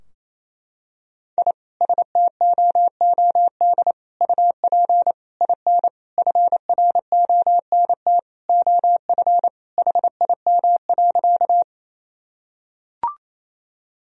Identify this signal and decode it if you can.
Morse code, heard as a keyed tone, about 28 wpm, 700 Hz. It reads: I STOOD UP IN FRONT OF HIM.
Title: Sentences from Top 1000 Words - Review Part 2 - 28wpm